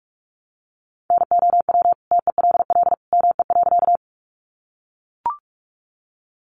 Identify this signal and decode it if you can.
Morse code, heard as a keyed tone, about 44 wpm, 700 Hz. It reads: NOW TELL ME.
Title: Sentences from Top 200 Words 45wpm